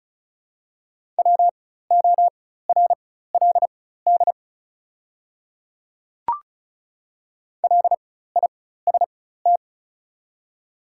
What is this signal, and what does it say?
Morse code, heard as a keyed tone, about 36 wpm, spaced as for 16 wpm, 700 Hz. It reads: WORLD LIST
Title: 4x Character Spacing - Top 500 Words - 35wpm